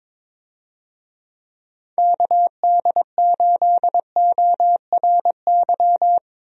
Morse code, heard as a keyed tone, about 22 wpm, 700 Hz. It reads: KD8ORY
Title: USA Call Signs - 6 Characters - Repeated 2x - 22wpm